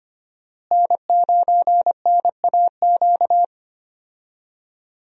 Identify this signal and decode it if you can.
Morse code, heard as a keyed tone, about 25 wpm, 700 Hz. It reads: N9NAQ
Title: USA Call Signs - 5 Characters - Repeated 3x - 25wpm